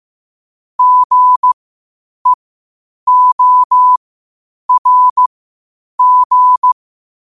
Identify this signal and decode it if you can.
Morse code, heard as a keyed tone, about 15 wpm, spaced as for 8 wpm, 1000 Hz. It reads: GEORG
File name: qotdfiles / 8wpm